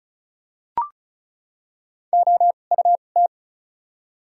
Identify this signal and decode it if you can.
Morse code, heard as a keyed tone, about 36 wpm, spaced as for 25 wpm, 700 Hz. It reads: OUT
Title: 2x Character Spacing - Top 100 Words - 35wpm